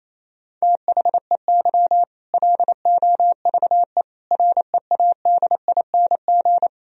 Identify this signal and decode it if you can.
Morse code, heard as a keyed tone, about 28 wpm, 700 Hz. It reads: THEY LOVE READING
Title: Sentences from Top 1000 Words - Review Part 3 - 28wpm